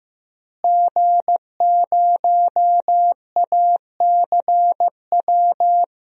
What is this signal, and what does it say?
Morse code, heard as a keyed tone, about 15 wpm, 700 Hz. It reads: G0ACW
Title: DXCC-100 Call Signs - 5 Characters - Repeated 3x - 15wpm